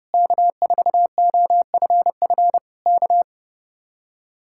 Morse code, heard as a keyed tone, about 37 wpm, 700 Hz. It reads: K4OFF K
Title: Speed-Racing Common QSO Phrases Encore - 30wpm